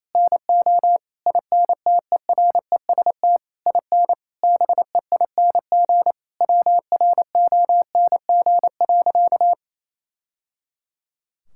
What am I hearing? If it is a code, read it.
Morse code, heard as a keyed tone, about 28 wpm, 700 Hz. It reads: NO INTEREST IN BEING WRONG.
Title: Sentences from Top 1000 Words - Review Part 6 - 28wpm